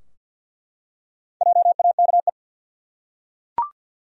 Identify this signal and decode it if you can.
Morse code, heard as a keyed tone, about 47 wpm, 700 Hz. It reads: JAKE
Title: USA Common Names - Max 4 Letters - 50wpm